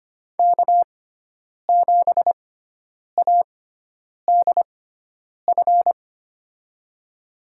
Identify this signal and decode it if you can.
Morse code, heard as a keyed tone, about 25 wpm, spaced as for 9 wpm, 700 Hz. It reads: K7ADF
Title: USA Call Signs - 5 Characters - 6x Character Spacing - 25wpm